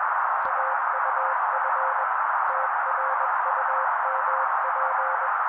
Morse code, heard as a keyed tone, about 21 wpm, 530 Hz. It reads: AUF TRUMP